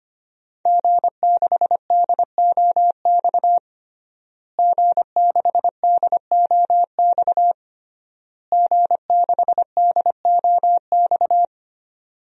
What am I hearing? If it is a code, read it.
Morse code, heard as a keyed tone, about 25 wpm, 700 Hz. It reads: G6DOX G6DOX G6DOX
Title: DXCC-100 Call Signs - 5 Characters - Repeated 3x - 25wpm